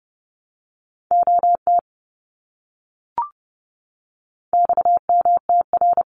Morse code, heard as a keyed tone, about 30 wpm, 700 Hz. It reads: OT XMTR
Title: Abbreviations Repeated 3x - 30wpm